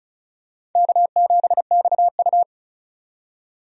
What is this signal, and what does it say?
Morse code, heard as a keyed tone, about 36 wpm, 690 Hz. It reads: K7XU
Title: USA Call Signs - 4 Characters - 35wpm